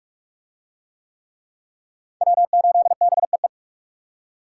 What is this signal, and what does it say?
Morse code, heard as a keyed tone, about 44 wpm, 700 Hz. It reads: W8BEE